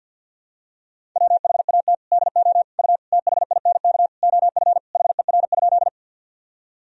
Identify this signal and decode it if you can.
Morse code, heard as a keyed tone, about 50 wpm, 700 Hz. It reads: WHAT DO U THINK OF HER?
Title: Sentences from Top 200 Words and Abbreviations 50wpm